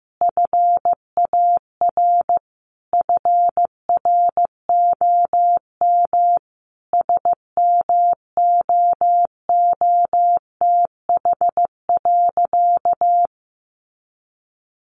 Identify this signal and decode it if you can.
Morse code, heard as a keyed tone, about 15 wpm, 700 Hz. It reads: FAR FROM SMOOTH.
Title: Sentences from Top 1400 Words 15wpm